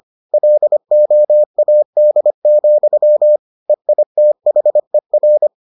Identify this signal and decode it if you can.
Morse code, heard as a keyed tone, about 25 wpm, 600 Hz. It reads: LOAD, EITHER